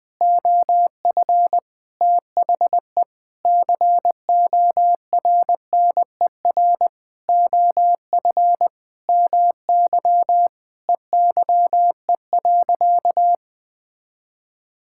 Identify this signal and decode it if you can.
Morse code, heard as a keyed tone, about 20 wpm, 700 Hz. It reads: OF THE CORNER OF MY EYE.
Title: Sentences with 7-Word Phrases - 20wpm